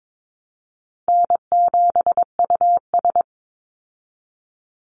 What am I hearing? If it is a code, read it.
Morse code, heard as a keyed tone, about 22 wpm, 700 Hz. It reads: N7US
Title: Call Signs Repeated 2x - 22wpm